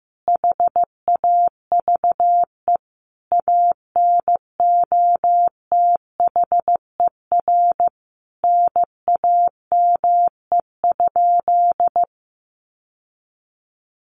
Morse code, heard as a keyed tone, about 15 wpm, 700 Hz. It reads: HAVE ANOTHER NAME?
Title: Sentences from Top 600 Words 15wpm